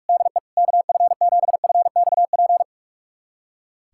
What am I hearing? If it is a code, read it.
Morse code, heard as a keyed tone, about 46 wpm, 700 Hz. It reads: DE KF7FXP
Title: USA Callsign DE Callsign - 6 Characters - 45wpm